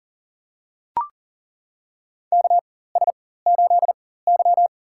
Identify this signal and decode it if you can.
Morse code, heard as a keyed tone, about 40 wpm, spaced as for 19 wpm, 700 Hz. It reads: KS8Y
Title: USA Call Signs - 4 Characters - 4x Character Spacing - 40wpm